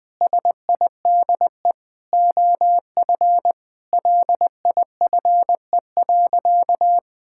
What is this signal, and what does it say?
Morse code, heard as a keyed tone, about 21 wpm, 700 Hz. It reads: SIDE OF LIFE.
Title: Sentences with 8-Word Phrases - 20wpm